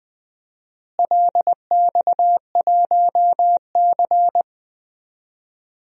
Morse code, heard as a keyed tone, about 21 wpm, 700 Hz. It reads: LX1C
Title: DXCC-100 Call Signs - 4 Characters - Repeated 3x - 20wpm